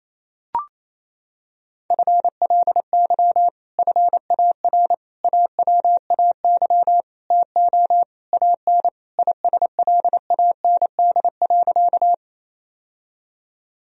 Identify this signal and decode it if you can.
Morse code, heard as a keyed tone, about 27 wpm, 700 Hz. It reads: FLY FAR AWAY TO AN ISLAND.
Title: Sentences from Top 1000 Words - Review Part 2 - 28wpm